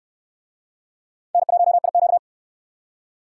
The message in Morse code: N1IQ

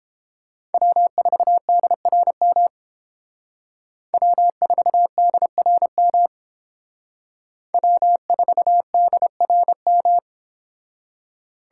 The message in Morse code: W4DRM W4DRM W4DRM

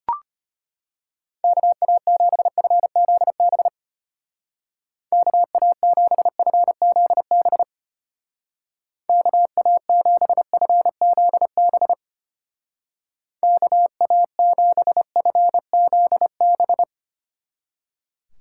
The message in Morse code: KA7FZB KA7FZB KA7FZB KA7FZB

